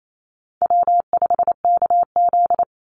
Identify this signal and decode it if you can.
Morse code, heard as a keyed tone, about 28 wpm, 700 Hz. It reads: W5KZ
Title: USA Call Signs 28wpm